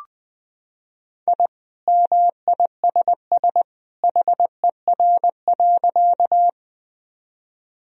Morse code, heard as a keyed tone, about 19 wpm, 700 Hz. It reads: I MISS HER.
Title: Sentences from Top 1000 Words - Review Part 3 - 20wpm